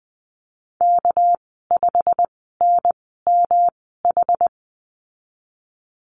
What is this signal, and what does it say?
Morse code, heard as a keyed tone, about 20 wpm, spaced as for 15 wpm, 700 Hz. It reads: K5NMH
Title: 207 Lesson 33g - Letter j - Call Signs 20wpm